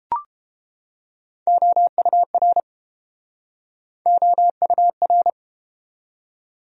Morse code, heard as a keyed tone, about 31 wpm, 700 Hz. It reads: OUR OUR